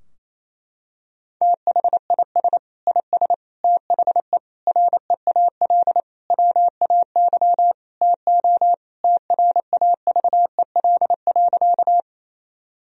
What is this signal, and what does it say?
Morse code, heard as a keyed tone, about 28 wpm, 700 Hz. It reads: THIS IS THE REAL WAY TO TRAVEL.
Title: Sentences from Top 1000 Words - Review Part 2 - 28wpm